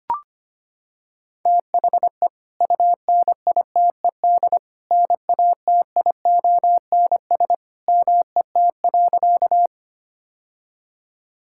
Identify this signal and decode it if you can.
Morse code, heard as a keyed tone, about 25 wpm, 700 Hz. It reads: THE UNITED NATIONS MET.
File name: practice / 25wpm / Sentences with 2-Word Phrases - 25wpm